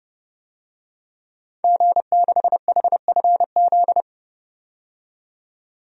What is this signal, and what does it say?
Morse code, heard as a keyed tone, about 31 wpm, 690 Hz. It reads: G6HFZ
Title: DXCC-100 Call Signs - 5 Characters - 30wpm